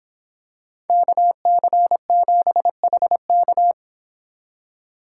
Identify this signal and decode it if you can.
Morse code, heard as a keyed tone, about 25 wpm, 700 Hz. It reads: KC7HK